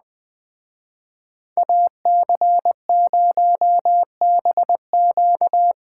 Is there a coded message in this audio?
Yes: Morse code from 1.6 to 5.7 s, heard as a keyed tone, about 20 wpm, 700 Hz. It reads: AC0BQ